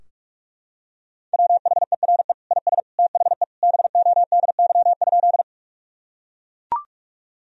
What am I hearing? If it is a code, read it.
Morse code, heard as a keyed tone, about 41 wpm, 700 Hz. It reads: WHERE IS THE BODY?
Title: Sentences from Top 1000 Words - Review Part 2 - 45wpm